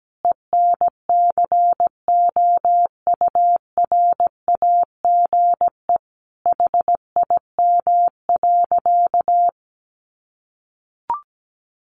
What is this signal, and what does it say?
Morse code, heard as a keyed tone, about 17 wpm, 700 Hz. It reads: ENCOURAGE HIM.
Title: Sentences from Top 1400 Words 17wpm